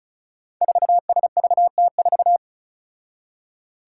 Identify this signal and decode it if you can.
Morse code, heard as a keyed tone, about 34 wpm, 700 Hz. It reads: 4SVT4
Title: Five Challenging Letter-Number Combinations 35wpm